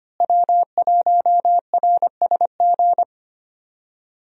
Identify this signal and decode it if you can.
Morse code, heard as a keyed tone, about 25 wpm, 700 Hz. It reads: W1RSG
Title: USA Call Signs - 5 Characters - Repeated 3x - 25wpm